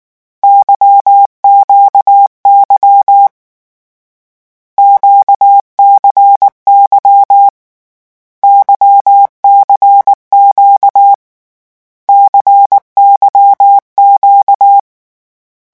Morse code, heard as a keyed tone, about 19 wpm, 800 Hz. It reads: YQY QCY YCQ CYQ
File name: morse mp3 / lecon2-vitesse 18